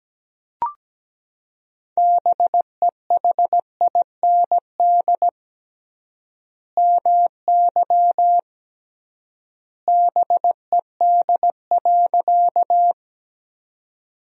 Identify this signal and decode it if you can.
Morse code, heard as a keyed tone, about 17 wpm, 700 Hz. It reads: BEHIND MY BED.
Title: 3x Word-Spacing - Sentences from Top 500 Words - 17wpm